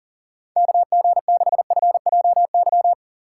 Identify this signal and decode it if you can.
Morse code, heard as a keyed tone, about 40 wpm, 700 Hz. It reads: KG6FJY
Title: USA Call Signs - 6 Characters - 3x Character Spacing - 40wpm